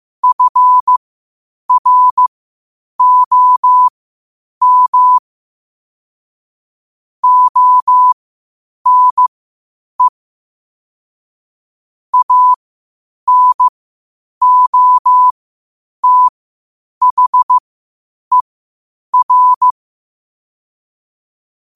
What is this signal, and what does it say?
Morse code, heard as a keyed tone, about 15 wpm, spaced as for 8 wpm, 1000 Hz. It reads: FROM ONE ANOTHER
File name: qotdfiles / 8wpm